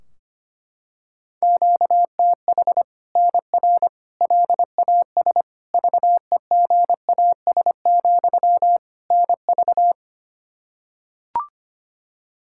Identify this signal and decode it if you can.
Morse code, heard as a keyed tone, about 25 wpm, 700 Hz. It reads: QTH NR LAS VEGAS, NV
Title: Practice QSO - Exchange 1 - Beginning - 25wpm